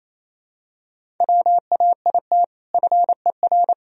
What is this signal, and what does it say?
Morse code, heard as a keyed tone, about 28 wpm, 700 Hz. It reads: WAIT FER
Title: Sentences from Top 500 Words and Abbreviations 28wpm